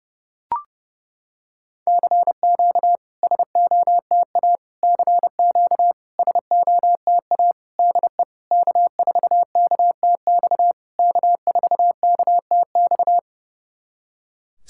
Morse code, heard as a keyed tone, about 30 wpm, 690 Hz. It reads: CQ SOTA CQ SOTA DE K4KTX K4KTX
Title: SOTA QSO Practice 30wpm